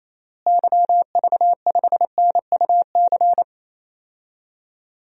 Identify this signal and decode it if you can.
Morse code, heard as a keyed tone, about 28 wpm, 700 Hz. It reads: YV5NUC
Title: DXCC-100 Call Signs - 6 Characters - Repeated 2x - 28wpm